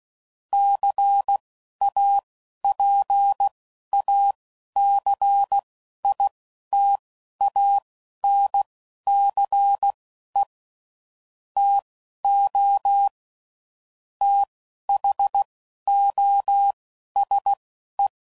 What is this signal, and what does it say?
Morse code, heard as a keyed tone, about 16 wpm, spaced as for 11 wpm, 800 Hz. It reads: CAPACITANCE TO THOSE
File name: CW Code Groups @10WPM / Technical Subjects / XC